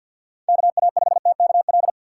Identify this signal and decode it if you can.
Morse code, heard as a keyed tone, about 52 wpm, 700 Hz. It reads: KA5TKL
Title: USA Call Signs - 4x Character Spacing - 50wpm